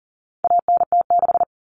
Morse code, heard as a keyed tone, about 38 wpm, 700 Hz. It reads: ANT6
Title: Common Cut Numbers - 4-Digit 40wpm